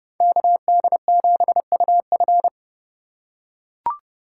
Morse code, heard as a keyed tone, about 30 wpm, 700 Hz. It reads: KD7UF
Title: USA Call Signs - 5 Characters - 4x Character Spacing - 30wpm